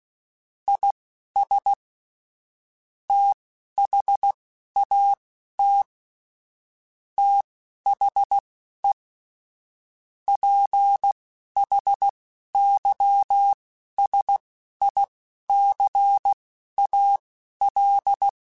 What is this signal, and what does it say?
Morse code, heard as a keyed tone, about 16 wpm, spaced as for 11 wpm, 800 Hz. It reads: IS THAT THE PHYSICAL
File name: Technical Subjects / Technical Subjects-mp3s / Sw Pwr Sup @10